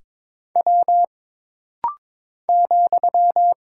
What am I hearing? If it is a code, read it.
Morse code, heard as a keyed tone, about 22 wpm, 700 Hz. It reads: W ,
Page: This rapid-fire timing is geared to push your ICR (Instant Character Recognition) skills.